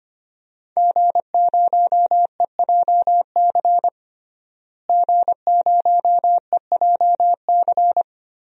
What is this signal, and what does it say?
Morse code, heard as a keyed tone, about 25 wpm, 700 Hz. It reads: G0EJC G0EJC
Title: DXCC-100 Call Signs - 5 Characters - Repeated 3x - 25wpm